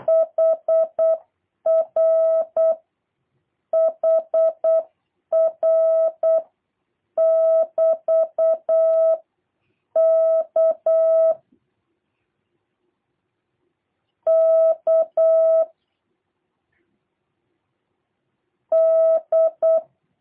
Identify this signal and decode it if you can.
Morse code, heard as a keyed tone, about 8 wpm, 650 Hz. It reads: HR HR <BT> K K D
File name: jh8cbh / 2023cw / review1-3